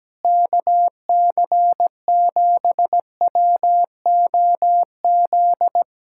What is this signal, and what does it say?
Morse code, heard as a keyed tone, about 17 wpm, 700 Hz. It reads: KC7WOZ
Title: Speed-Racing Common QSO Phrases Encore - 17wpm